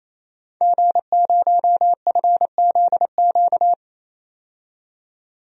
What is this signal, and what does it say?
Morse code, heard as a keyed tone, about 28 wpm, 700 Hz. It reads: G0FZQ
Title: DXCC-100 Call Signs - 5 Characters - 28wpm